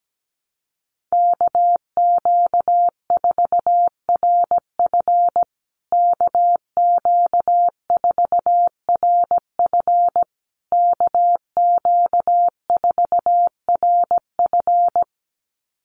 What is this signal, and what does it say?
Morse code, heard as a keyed tone, about 17 wpm, 700 Hz. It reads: KQ4RF KQ4RF KQ4RF
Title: USA Call Signs Repeated 3x 17wpm